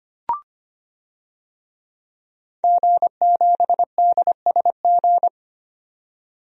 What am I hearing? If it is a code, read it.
Morse code, heard as a keyed tone, about 25 wpm, 700 Hz. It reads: G7DSG
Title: DXCC-100 Call Signs - 5 Characters - 25wpm